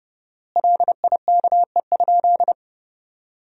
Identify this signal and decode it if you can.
Morse code, heard as a keyed tone, about 32 wpm, 700 Hz. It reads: LIKE?
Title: 4x Word-Spacing - Sentences from Top 100 Words - 30wpm